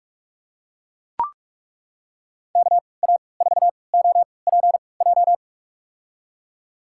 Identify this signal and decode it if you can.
Morse code, heard as a keyed tone, about 42 wpm, spaced as for 25 wpm, 700 Hz. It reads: KA4OPJ
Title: USA Call Signs - 3x Character Spacing - 45wpm